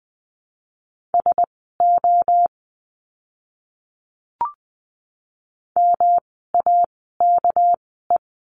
Morse code, heard as a keyed tone, about 20 wpm, spaced as for 14 wpm, 700 Hz. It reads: SO MAKE
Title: 230 Lesson 37f - Letter x - Words 20wpm